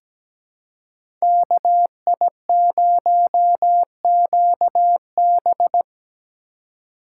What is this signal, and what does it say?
Morse code, heard as a keyed tone, about 17 wpm, 700 Hz. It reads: KI0QB
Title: USA Call Signs - 3x Character Spacing - 17wpm